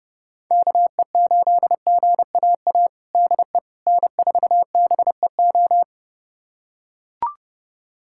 Morse code, heard as a keyed tone, about 30 wpm, 700 Hz. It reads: KE8GAA DE N4BEO
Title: USA Callsign DE Callsign 30wpm